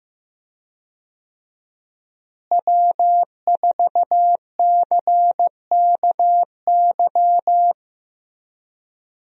Morse code, heard as a keyed tone, about 15 wpm, 700 Hz. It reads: W4CKY
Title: USA Call Signs - 5 Characters - Repeated 3x - 15wpm